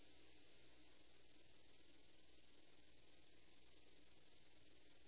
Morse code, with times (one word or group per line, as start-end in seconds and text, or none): none